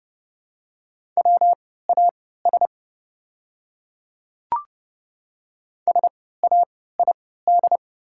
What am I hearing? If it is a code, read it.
Morse code, heard as a keyed tone, about 32 wpm, spaced as for 17 wpm, 700 Hz. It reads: WAS SAID
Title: 3x Character Spacing - Top 500 Words - 30wpm